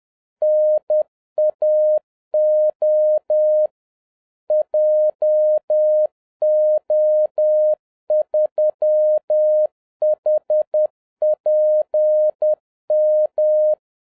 Morse code, heard as a keyed tone, about 10 wpm, 610 Hz. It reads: NAO JO3HPM